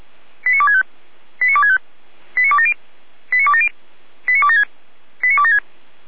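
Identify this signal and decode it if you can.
Five-tone selective call (ZVEI), heard as quick runs of five tones, tones of about 70 ms. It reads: 89265 89265 89270 89270 89276 89276
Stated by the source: ZVEI five-tone call received by SDR, Feuerwehr Orlamünde fire brigade.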